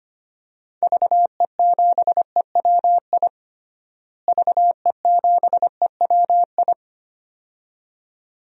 Morse code, heard as a keyed tone, about 25 wpm, 700 Hz. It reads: VE7EWI VE7EWI
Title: DXCC-100 Call Signs - 6 Characters - Repeated 2x - 25wpm